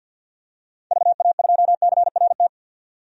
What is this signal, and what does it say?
Morse code, heard as a keyed tone, about 52 wpm, 700 Hz. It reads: VA2CRT
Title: DXCC-100 Call Signs - 6 Characters - 50wpm